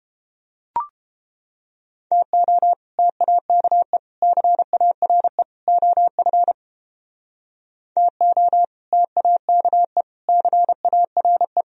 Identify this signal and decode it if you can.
Morse code, heard as a keyed tone, about 30 wpm, 700 Hz. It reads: TO TAKE CARE OF TO TAKE CARE